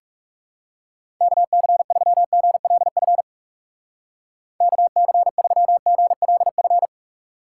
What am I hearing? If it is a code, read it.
Morse code, heard as a keyed tone, about 37 wpm, 700 Hz. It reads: KC3GLF KC3GLF
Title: Speed Racing - USA Call Signs 30wpm